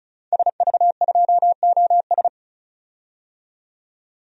Morse code, heard as a keyed tone, about 34 wpm, 700 Hz. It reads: SV2OS
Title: DXCC-100 Call Signs - 5 Characters - 35wpm